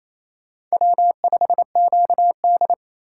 Morse code, heard as a keyed tone, about 34 wpm, 700 Hz. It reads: W5QD
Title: Speed Racing - USA 1x2 Call Signs 28wpm